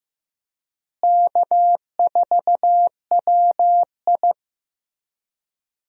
Morse code, heard as a keyed tone, about 15 wpm, 700 Hz. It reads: K4WI
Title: USA Call Signs - 4 Characters - Repeated 2x - 15wpm